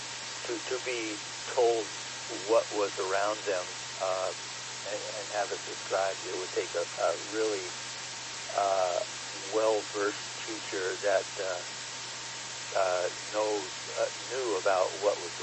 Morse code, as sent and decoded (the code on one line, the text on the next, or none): none